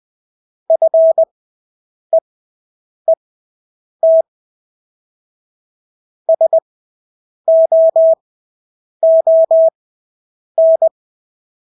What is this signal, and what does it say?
Morse code, heard as a keyed tone, about 20 wpm, spaced as for 8 wpm, 650 Hz. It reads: FEET SOON
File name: Engelske ord 20/8 wpm